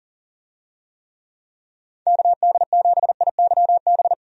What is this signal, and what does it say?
Morse code, heard as a keyed tone, about 37 wpm, 700 Hz. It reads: KD7IYB
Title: USA Call Signs - 6 Characters - 40wpm